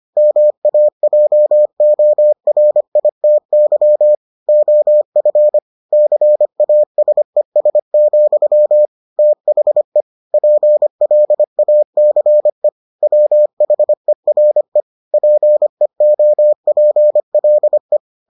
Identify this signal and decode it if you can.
Morse code, heard as a keyed tone, about 25 wpm, 600 Hz. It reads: MAJORITY OF CASES, THE PLACE WHERE PEOPLE